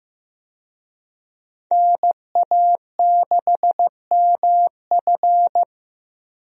Morse code, heard as a keyed tone, about 15 wpm, 700 Hz. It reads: NA6MF
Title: USA Call Signs - 5 Characters - Repeated 2x - 15wpm